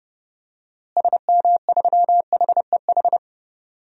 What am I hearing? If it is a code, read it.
Morse code, heard as a keyed tone, about 30 wpm, 700 Hz. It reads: SM3HEH
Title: DXCC-100 Call Signs - 6 Characters - Repeated 2x - 30wpm